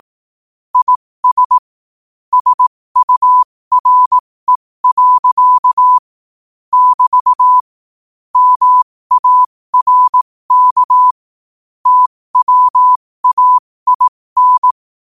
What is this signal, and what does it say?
Morse code, heard as a keyed tone, about 18 wpm, spaced as for 15 wpm, 1000 Hz. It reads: IS SURE. <BT> MARK TWAIN
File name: qotdfiles / 15wpm